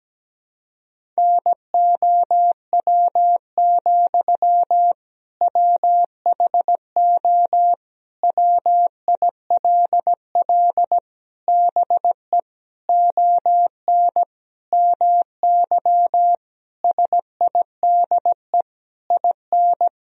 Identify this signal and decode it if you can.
Morse code, heard as a keyed tone, about 17 wpm, 700 Hz. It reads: NOW, WHO WILL BE ON MY SIDE IN
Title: Sentences from Top 1000 Words - Review Part 1 - 17wpm